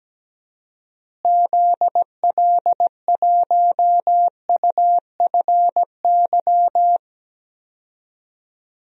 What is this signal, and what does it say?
Morse code, heard as a keyed tone, about 17 wpm, 700 Hz. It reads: ZL1UFY
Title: DXCC-100 Call Signs - 6 Characters - Repeated 2x - 17wpm